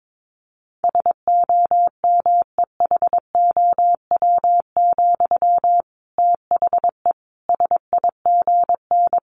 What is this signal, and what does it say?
Morse code, heard as a keyed tone, about 22 wpm, 700 Hz. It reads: SOMEHOW, THE SIGN
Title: A Sound Of Thunder 22wpm